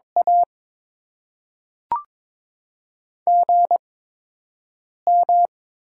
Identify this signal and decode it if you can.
Morse code, heard as a keyed tone, about 22 wpm, spaced as for 6 wpm, 700 Hz. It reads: A GM
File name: Top 20 QSO Elements - 8x Character Spacing - 22wpm